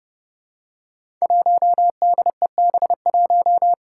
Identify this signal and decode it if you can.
Morse code, heard as a keyed tone, about 30 wpm, 700 Hz. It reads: 1DEB1
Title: Five Challenging Letter-Number Combinations 30wpm